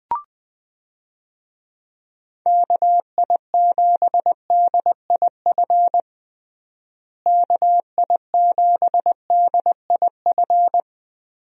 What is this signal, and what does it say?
Morse code, heard as a keyed tone, about 21 wpm, 700 Hz. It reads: KI7DIF KI7DIF
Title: USA Call Signs - 6 Characters - Repeated 3x - 20wpm